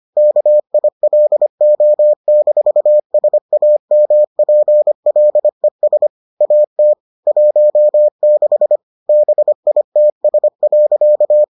Morse code, heard as a keyed tone, about 25 wpm, 600 Hz. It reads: KILO-SAMPLES AT 16 BITS.